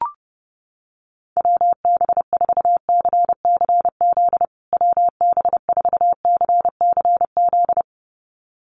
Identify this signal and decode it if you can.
Morse code, heard as a keyed tone, about 30 wpm, 700 Hz. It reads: WB4CCZ WB4CCZ
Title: USA Call Signs Repeated 2x 30wpm